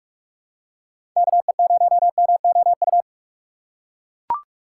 Morse code, heard as a keyed tone, about 41 wpm, 700 Hz. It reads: KE0MOU
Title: USA Call Signs - 6 Characters - 6x Character Spacing - 45wpm